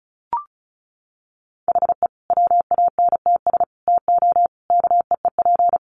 Morse code, heard as a keyed tone, about 34 wpm, 700 Hz. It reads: HE WANTS TO KEEP
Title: Additional Easy Phrases 35wpm